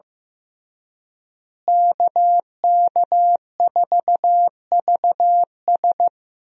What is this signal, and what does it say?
Morse code, heard as a keyed tone, about 15 wpm, 700 Hz. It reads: KK4VS
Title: USA Call Signs - 5 Characters - Repeated 3x - 15wpm